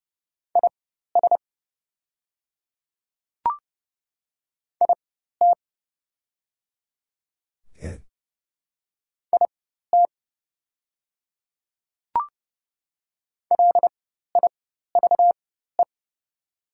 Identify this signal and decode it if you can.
Morse code, heard as a keyed tone, about 30 wpm, spaced as for 14 wpm, 700 Hz. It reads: IS IT IT LIVE
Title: 4x Character Spacing - Top 200 Words - 30wpm